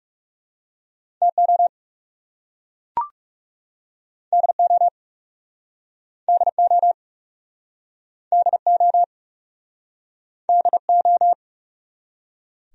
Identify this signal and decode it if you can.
Morse code, heard as a keyed tone, about 37 wpm, 700 Hz. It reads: TO DO DO DO DO
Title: Speed-Racing Top 500 Words 30wpm